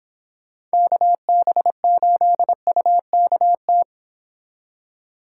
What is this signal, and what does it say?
Morse code, heard as a keyed tone, about 25 wpm, 700 Hz. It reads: KB8UKT